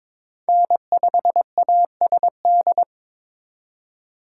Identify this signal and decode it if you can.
Morse code, heard as a keyed tone, about 26 wpm, 700 Hz. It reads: N5ASD
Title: Speed Racing - USA 1x3 Call Signs 22wpm